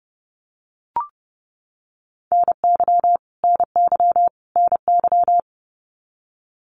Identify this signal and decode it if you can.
Morse code, heard as a keyed tone, about 30 wpm, 700 Hz. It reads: NY NY NY
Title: US States Repeated 3x 30wpm